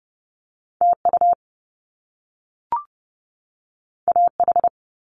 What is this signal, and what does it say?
Morse code, heard as a keyed tone, about 30 wpm, 700 Hz. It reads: TU AH
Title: Top Two Letter Combinations in Words 30wpm